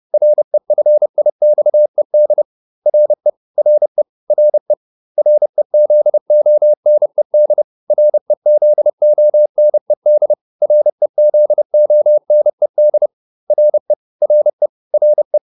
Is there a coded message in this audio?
Yes: Morse code from 0.1 to 15.4 s, heard as a keyed tone, about 30 wpm, 600 Hz. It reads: REFIXED RE RE RE REZONED REZONED REZONED RE RE RE